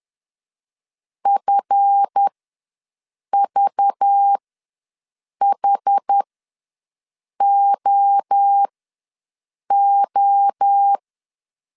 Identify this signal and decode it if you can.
Morse code, heard as a keyed tone, about 10 wpm, spaced as for 6 wpm, 800 Hz. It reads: FVHOO